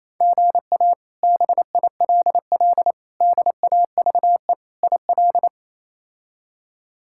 Morse code, heard as a keyed tone, about 28 wpm, 700 Hz. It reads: GA BILL DAVE IL
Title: K1USN SST QSO Practice - 28wpm